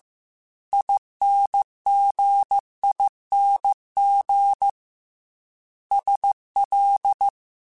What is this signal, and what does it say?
Morse code, heard as a keyed tone, about 15 wpm, 800 Hz. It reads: INGING SL